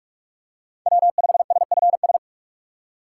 Morse code, heard as a keyed tone, about 47 wpm, 700 Hz. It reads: W5SFS
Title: USA Call Signs - 5 Characters - 6x Character Spacing - 45wpm